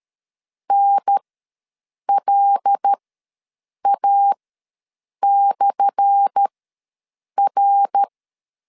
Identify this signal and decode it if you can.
Morse code, heard as a keyed tone, about 12 wpm, spaced as for 7 wpm, 800 Hz. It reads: NLA/R
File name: LEZIONI / 10 SIGNS / 10A SLASH ws 12wpm 3443